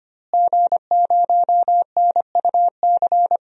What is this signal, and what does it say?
Morse code, heard as a keyed tone, about 25 wpm, 700 Hz. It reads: G0NUC
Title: DXCC-100 Call Signs - 5 Characters - Repeated 2x - 25wpm